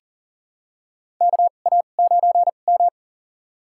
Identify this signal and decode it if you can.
Morse code, heard as a keyed tone, about 40 wpm, spaced as for 29 wpm, 700 Hz. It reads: KA9M
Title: USA Call Signs - 2x Character Spacing - 40wpm